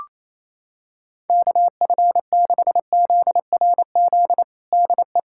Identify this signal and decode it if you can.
Morse code, heard as a keyed tone, about 28 wpm, 700 Hz. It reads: KF6ZRZ DE
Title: USA Callsign DE Callsign 28wpm